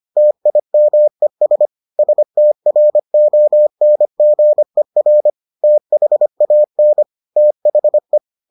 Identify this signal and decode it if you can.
Morse code, heard as a keyed tone, about 25 wpm, 600 Hz. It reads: TIMES STRONGER THAN THE